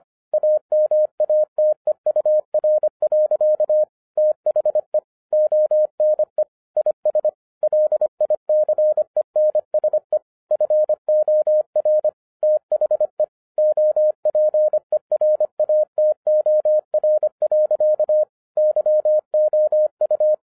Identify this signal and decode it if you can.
Morse code, heard as a keyed tone, about 25 wpm, 610 Hz. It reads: AMATEUR. THE ONE IS LICENSE FOR THE OPERATOR. YOU